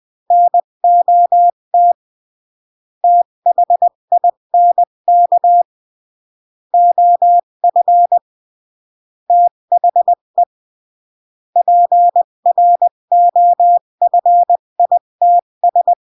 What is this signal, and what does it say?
Morse code, heard as a keyed tone, about 20 wpm, 700 Hz. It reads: NOT THINK OF THE PROFITS
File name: CW / jodorowsky / 82 commandments 20-15